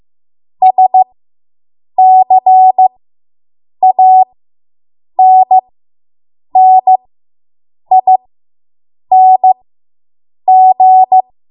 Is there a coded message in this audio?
Yes: Morse code from 0.6 to 11.2 s, heard as a keyed tone, about 15 wpm, spaced as for 7 wpm, 750 Hz. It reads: SCANNING